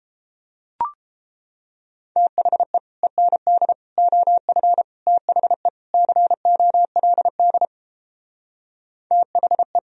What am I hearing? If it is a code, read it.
Morse code, heard as a keyed tone, about 29 wpm, 700 Hz. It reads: THE END OF THE COLD THE